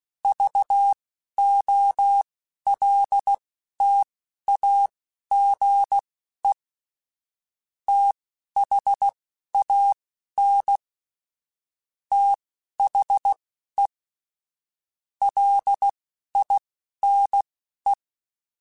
Morse code, heard as a keyed tone, about 16 wpm, spaced as for 11 wpm, 800 Hz. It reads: VOLTAGE THAN THE LINE
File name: Technical Subjects / Technical Subjects-mp3s / Sw Pwr Sup @10